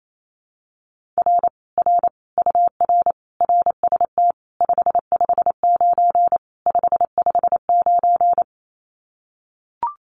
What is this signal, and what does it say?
Morse code, heard as a keyed tone, about 28 wpm, 700 Hz. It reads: R R UR RST 559 559
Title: Common QSO Phrases Encore 28wpm